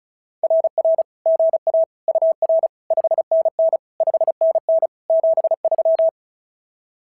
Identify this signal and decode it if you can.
Morse code, heard as a keyed tone, about 36 wpm, 650 Hz. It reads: RR GA UR 5NN 5NN 73
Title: SOTA QSO Practice 35wpm